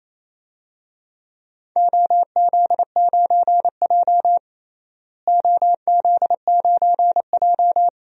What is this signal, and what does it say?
Morse code, heard as a keyed tone, about 28 wpm, 700 Hz. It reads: OZ9J OZ9J
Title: DXCC-100 Call Signs - 4 Characters - Repeated 2x - 28wpm